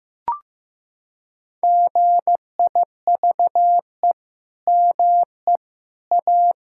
Morse code, heard as a keyed tone, about 16 wpm, 700 Hz. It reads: GIVE ME A
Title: Sentences from Top 1600 Words 15wpm